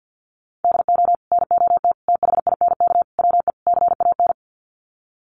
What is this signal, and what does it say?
Morse code, heard as a keyed tone, about 50 wpm, 700 Hz. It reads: DO NOT THINK WE CAN
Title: Sets of 5 Words Encore2 50wpm